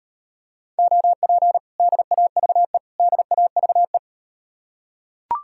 Morse code, heard as a keyed tone, about 38 wpm, 700 Hz. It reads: OP DAVE DAVE